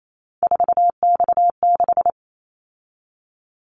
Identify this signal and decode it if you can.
Morse code, heard as a keyed tone, about 28 wpm, 700 Hz. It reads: 4X6
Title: Callsign Prefixes for Top 50 DXCC - 28wpm